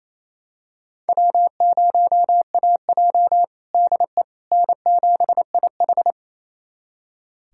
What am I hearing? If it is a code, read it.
Morse code, heard as a keyed tone, about 28 wpm, 700 Hz. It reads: W0AJ DE N7IH
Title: USA Callsign DE Callsign - 4 Characters - 28wpm